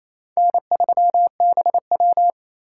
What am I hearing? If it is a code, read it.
Morse code, heard as a keyed tone, about 36 wpm, 700 Hz. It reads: N3BW